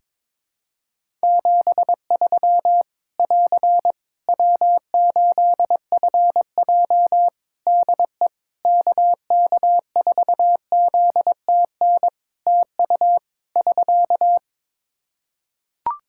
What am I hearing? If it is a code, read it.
Morse code, heard as a keyed tone, about 22 wpm, 700 Hz. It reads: 73 <AR> W8FJ DE KK4ZTN TU <SK>
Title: Practice QSO - Exchange 4 - Ending - 22wpm